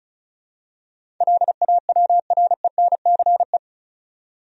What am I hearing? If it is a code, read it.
Morse code, heard as a keyed tone, about 36 wpm, 700 Hz. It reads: LAWRENCE
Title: USA Common Names - 35wpm